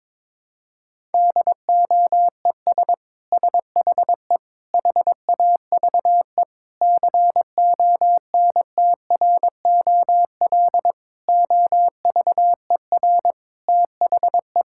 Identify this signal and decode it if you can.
Morse code, heard as a keyed tone, about 22 wpm, 700 Hz. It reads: DOES SHE HAVE CONTROL OVER THE
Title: Sentences from Top 1000 Words - Review Part 5 - 22wpm